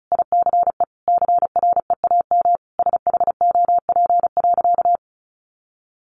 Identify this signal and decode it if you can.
Morse code, heard as a keyed tone, about 34 wpm, 700 Hz. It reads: ICE CREAM SHOP.
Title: Easy Phrases 35wpm